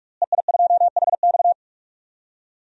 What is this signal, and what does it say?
Morse code, heard as a keyed tone, about 46 wpm, 700 Hz. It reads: EI2HX
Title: DXCC-100 Call Signs - 5 Characters - Repeated 2x - 45wpm